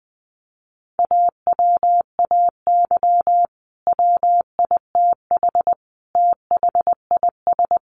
Morse code, heard as a keyed tone, about 19 wpm, 700 Hz. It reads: AWAY WITH THIS
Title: Sentences from Top 1100 Words 20wpm